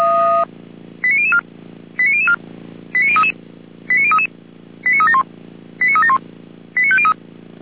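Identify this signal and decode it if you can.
Five-tone selective call (ZVEI), heard as quick runs of five tones, tones of about 70 ms. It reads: 89004 89004 89033 89033 89371 89371 89503
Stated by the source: ZVEI five-tone call received by SDR, from fire brigade Feuerwehr Orlamünde.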